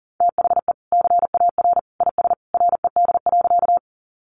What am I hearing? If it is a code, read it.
Morse code, heard as a keyed tone, about 40 wpm, 700 Hz. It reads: THE CAR IS RED.
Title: Easy Phrases 40wpm